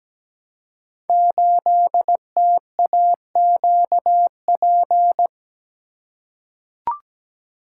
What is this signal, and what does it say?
Morse code, heard as a keyed tone, about 17 wpm, 700 Hz. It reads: 8TAQP
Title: Code Groups - 5 Letter:Numbers - 17wpm